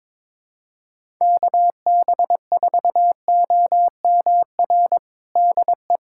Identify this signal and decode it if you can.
Morse code, heard as a keyed tone, about 22 wpm, 700 Hz. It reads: KB4OMR DE
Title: USA Callsign DE Callsign - 6 Characters - 22wpm